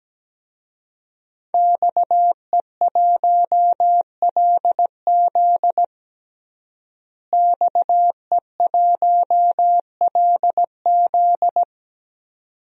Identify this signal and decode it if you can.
Morse code, heard as a keyed tone, about 17 wpm, 700 Hz. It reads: XE1LZ XE1LZ
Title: DXCC-100 Call Signs - 5 Characters - Repeated 2x - 17wpm